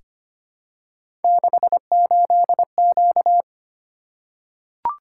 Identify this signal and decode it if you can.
Morse code, heard as a keyed tone, about 25 wpm, 700 Hz. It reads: 68Q
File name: Three Letter-Number Combinations - 25wpm